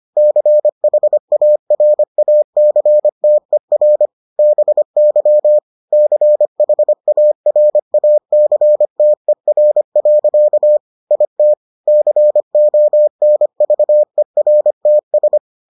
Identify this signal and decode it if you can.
Morse code, heard as a keyed tone, about 25 wpm, 600 Hz. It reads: CHARACTER BY CHARACTER. IT CONVERTS